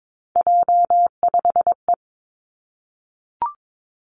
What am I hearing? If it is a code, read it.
Morse code, heard as a keyed tone, about 22 wpm, 700 Hz. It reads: J5E
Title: Three Challenging Letter-Number Combinations 22wpm